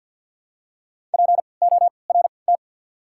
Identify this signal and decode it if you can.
Morse code, heard as a keyed tone, about 53 wpm, spaced as for 29 wpm, 690 Hz. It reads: PORT
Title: 3x Character Spacing - Top 500 Words - 50wpm